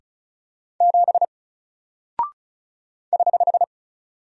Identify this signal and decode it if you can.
Morse code, heard as a keyed tone, about 34 wpm, 700 Hz. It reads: 7 <HH>
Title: Single Character (Letter:Number:Punctuation:Prosign) Alt 35wpm